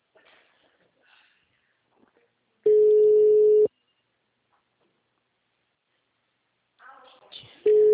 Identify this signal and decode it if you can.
Ringback tone (Europe, 425 Hz), with speech.